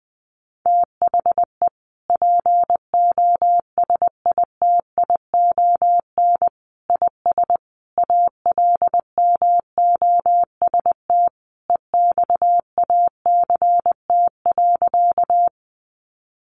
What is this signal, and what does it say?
Morse code, heard as a keyed tone, about 20 wpm, 700 Hz. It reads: THE POSITION IS ALMOST EXACT.
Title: Sentences from Top 1300 Words 20wpm